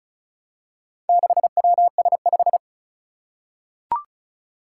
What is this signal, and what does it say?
Morse code, heard as a keyed tone, about 36 wpm, 700 Hz. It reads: 6WS5
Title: Four Challenging Letter-Number Combinations 35wpm